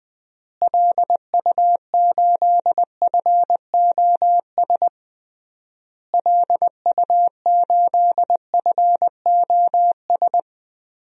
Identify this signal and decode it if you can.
Morse code, heard as a keyed tone, about 20 wpm, 700 Hz. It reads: LU8FOS LU8FOS
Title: DXCC-100 Call Signs - 6 Characters - Repeated 3x - 20wpm